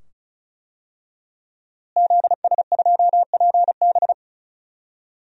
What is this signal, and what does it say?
Morse code, heard as a keyed tone, about 36 wpm, 700 Hz. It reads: ZS2PB
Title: DXCC-100 Call Signs - 5 Characters - Repeated 2x - 35wpm